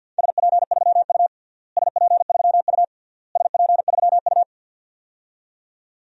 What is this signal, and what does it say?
Morse code, heard as a keyed tone, about 46 wpm, 700 Hz. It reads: SP3U SP3U SP3U
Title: DXCC-100 Call Signs - 4 Characters - Repeated 3x - 50wpm